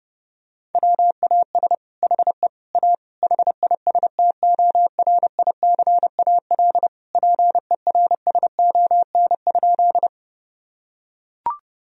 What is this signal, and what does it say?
Morse code, heard as a keyed tone, about 29 wpm, 700 Hz. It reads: WAS HE A HISTORICAL PERSON?
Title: Sentences from Top 1000 Words - Review Part 4 - 30wpm